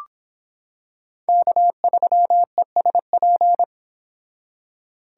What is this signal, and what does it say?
Morse code, heard as a keyed tone, about 25 wpm, 700 Hz. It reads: K3ESP